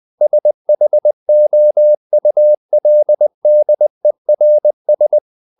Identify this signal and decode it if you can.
Morse code, heard as a keyed tone, about 20 wpm, 600 Hz. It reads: SHOULDERS
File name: the old man and the sea part 2 20wpm